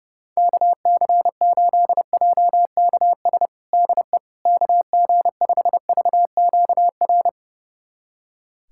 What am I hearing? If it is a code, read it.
Morse code, heard as a keyed tone, about 32 wpm, 700 Hz. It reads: KC8JKS DE KG5VQR
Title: USA Callsign DE Callsign - 6 Characters - 30wpm